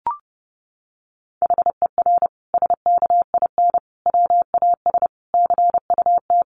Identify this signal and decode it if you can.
Morse code, heard as a keyed tone, about 30 wpm, 700 Hz. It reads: HER SKIN WAS CUT
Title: Sentences from Top 1300 Words 30wpm